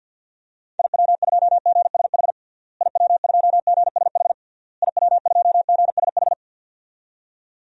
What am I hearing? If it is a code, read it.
Morse code, heard as a keyed tone, about 52 wpm, 700 Hz. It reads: IW2GSH IW2GSH IW2GSH
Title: DXCC-100 Call Signs - 6 Characters - Repeated 3x - 50wpm